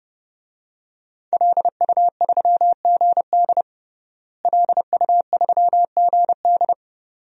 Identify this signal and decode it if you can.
Morse code, heard as a keyed tone, about 30 wpm, 700 Hz. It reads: LU3GD LU3GD
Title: DXCC-100 Call Signs - 5 Characters - Repeated 2x - 30wpm